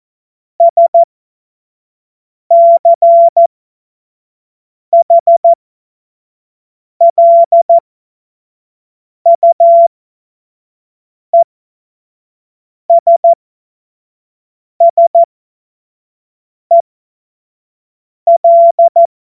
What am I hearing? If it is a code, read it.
Morse code, heard as a keyed tone, about 14 wpm, spaced as for 5 wpm, 680 Hz. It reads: SCHLUESSEL